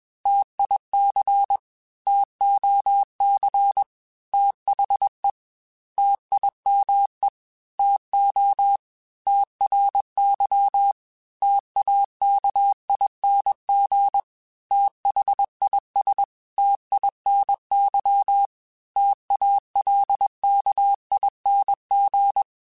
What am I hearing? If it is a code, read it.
Morse code, heard as a keyed tone, about 21 wpm, 800 Hz. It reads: TIC TOC THE TIME TO TRY TAKING THIS TINY TALKING
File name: T Runs / T Runs mp3s / T Runs @20